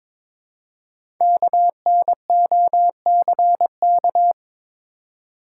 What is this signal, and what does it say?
Morse code, heard as a keyed tone, about 22 wpm, 700 Hz. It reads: KNOCK
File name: 5 Letter Words - First 3 Letters Given - 22wpm